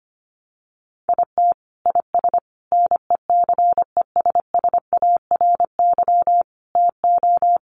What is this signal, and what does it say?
Morse code, heard as a keyed tone, about 25 wpm, 700 Hz. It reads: IT IS NECESSARY TO
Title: Sentences from Top 800 Words 25wpm